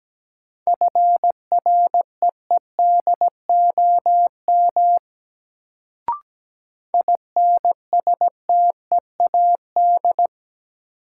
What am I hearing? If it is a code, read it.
Morse code, heard as a keyed tone, about 17 wpm, 700 Hz. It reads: FREEDOM INSTEAD
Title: Sentences from Top 800 Words - New Word Drill - 17wpm